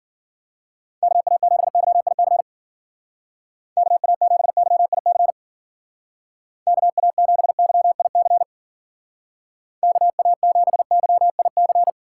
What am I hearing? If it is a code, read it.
Morse code, heard as a keyed tone, about 49 wpm, 700 Hz. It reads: KA7YIC KA7YIC KA7YIC KA7YIC
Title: Speed Racing - USA 2x3 Call Signs 40wpm